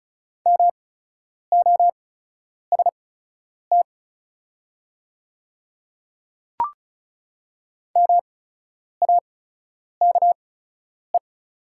Morse code, heard as a keyed tone, about 36 wpm, spaced as for 10 wpm, 700 Hz. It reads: MOST MAKE
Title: 8x Character Spacing - Top 100 Words - 35wpm